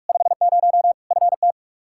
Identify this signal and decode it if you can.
Morse code, heard as a keyed tone, about 46 wpm, 700 Hz. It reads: 50 FT
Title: Practice QSO - Exchange 2 - Info - 45wpm